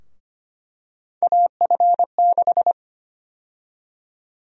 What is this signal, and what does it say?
Morse code, heard as a keyed tone, about 25 wpm, 700 Hz. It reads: AF6
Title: USA Call Sign Prefixes - 25wpm